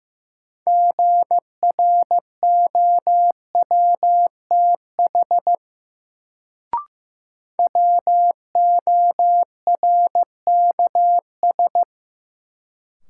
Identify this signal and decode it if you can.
Morse code, heard as a keyed tone, about 15 wpm, 700 Hz. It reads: GROWTH WORKS
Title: Sentences from Top 1000 Words - Review Part 5 - New Word Drill - 15wpm